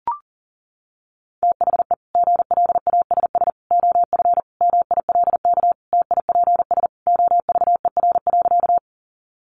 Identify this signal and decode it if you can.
Morse code, heard as a keyed tone, about 40 wpm, 700 Hz. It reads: THE GLASS OF MILK TIPS OVER.
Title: Additional Easy Phrases 40wpm